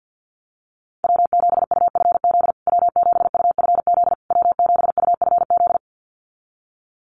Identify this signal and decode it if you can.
Morse code, heard as a keyed tone, about 48 wpm, 700 Hz. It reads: W7UFZ W7UFZ W7UFZ
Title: USA Call Signs Repeated 3x 50wpm